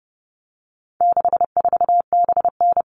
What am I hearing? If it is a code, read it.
Morse code, heard as a keyed tone, about 30 wpm, 700 Hz. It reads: 64BN
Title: All Cut Numbers - 4-Digit 30wpm